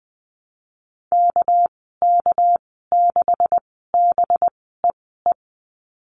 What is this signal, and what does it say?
Morse code, heard as a keyed tone, about 19 wpm, spaced as for 14 wpm, 700 Hz. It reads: KK6BEE
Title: 207 Lesson 33g - Letter j - Call Signs 20wpm